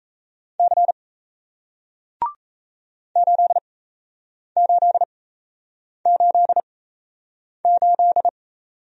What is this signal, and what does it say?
Morse code, heard as a keyed tone, about 37 wpm, spaced as for 8 wpm, 700 Hz. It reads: C 8888